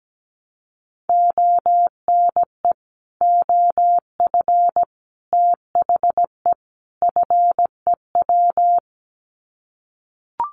Morse code, heard as a keyed tone, about 17 wpm, 700 Hz. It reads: ONE OF THE FEW
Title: Sets of 4 Words 17wpm